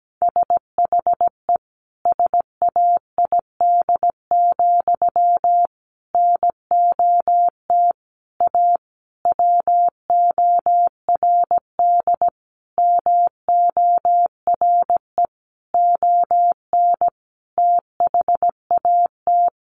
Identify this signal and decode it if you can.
Morse code, heard as a keyed tone, about 17 wpm, 700 Hz. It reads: SHE SAID, NOT A WORD MORE ON THAT
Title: Sentences from Top 800 Words 17wpm